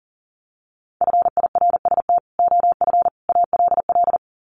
Morse code, heard as a keyed tone, about 40 wpm, 700 Hz. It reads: FIRST OF ALL
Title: Sets of 3 Words 40wpm